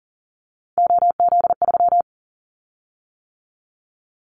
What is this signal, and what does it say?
Morse code, heard as a keyed tone, about 40 wpm, 700 Hz. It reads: OZ3
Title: Callsign Prefixes for Top 50 DXCC - 40wpm